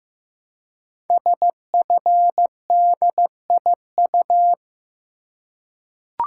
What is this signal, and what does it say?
Morse code, heard as a keyed tone, about 15 wpm, 700 Hz. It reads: SFDIU
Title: Code Groups - 5 Letters - 15wpm